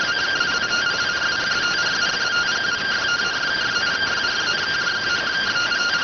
Two-tone teleprinter signal, about 170 Hz shift.